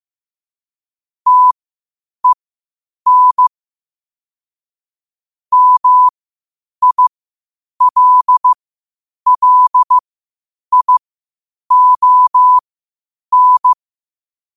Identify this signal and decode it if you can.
Morse code, heard as a keyed tone, about 15 wpm, spaced as for 8 wpm, 1000 Hz. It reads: TEN MILLION
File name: qotdfiles / 8wpm